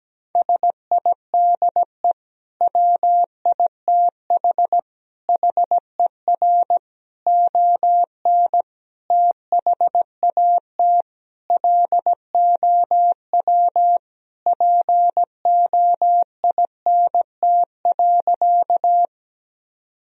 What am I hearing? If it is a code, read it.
Morse code, heard as a keyed tone, about 17 wpm, 700 Hz. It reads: SIDE WITH HER ON THAT LOW POINT.
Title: Sentences from Top 1000 Words - Review Part 1 - 17wpm